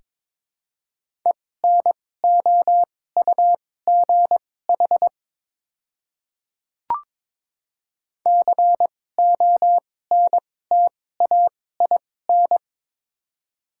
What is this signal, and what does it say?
Morse code, heard as a keyed tone, about 22 wpm, spaced as for 16 wpm, 700 Hz. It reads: ENOUGH CONTAIN
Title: 2x Character Spacing - Top 500 Words - 22wpm